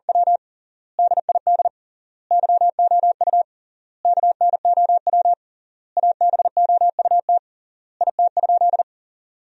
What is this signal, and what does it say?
Morse code, heard as a keyed tone, about 40 wpm, 700 Hz. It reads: W DID YOU KNOW ABOUT IT?
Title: 3x Word-Spacing - Sentences from Top 100 Words - 40wpm